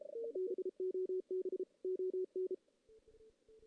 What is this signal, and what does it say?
Morse code, heard as a keyed tone, about 33 wpm, 370 Hz. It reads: BOB ON
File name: 2016 JAN CW NAQP / 2016 Jan naqp k5ot - 0-2